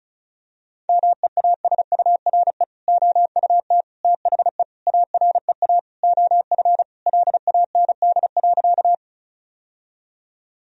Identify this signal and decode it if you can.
Morse code, heard as a keyed tone, about 34 wpm, 700 Hz. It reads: MEASURE OUT THE AREA OF LAND.
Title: Sentences from Top 1000 Words - Review Part 2 - 35wpm